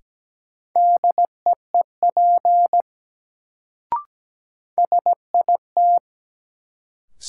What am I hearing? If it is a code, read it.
Morse code, heard as a keyed tone, about 17 wpm, 700 Hz. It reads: DEEP SIT
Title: Sentences from Top 500 Words - New Word Drill - 17wpm